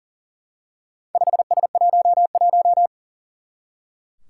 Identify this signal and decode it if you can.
Morse code, heard as a keyed tone, about 41 wpm, 700 Hz. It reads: 5S11
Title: Four Challenging Letter-Number Combinations 40wpm